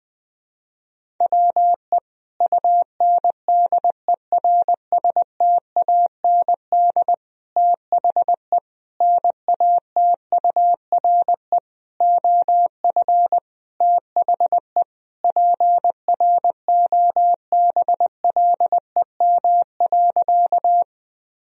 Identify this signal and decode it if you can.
Morse code, heard as a keyed tone, about 21 wpm, 700 Hz. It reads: WE UNDERSTAND THE NATURE OF THE PROBLEM.
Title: Sentences from Top 1000 Words - Review Part 2 - 20wpm